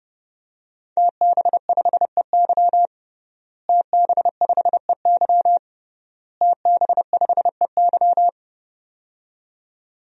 Morse code, heard as a keyed tone, about 30 wpm, 700 Hz. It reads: TB5EY TB5EY TB5EY
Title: DXCC-100 Call Signs - 5 Characters - Repeated 3x - 30wpm